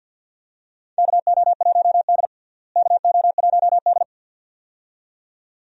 Morse code, heard as a keyed tone, about 51 wpm, 700 Hz. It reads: KO1D KO1D
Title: USA Call Signs - 4 Characters - Repeated 2x - 50wpm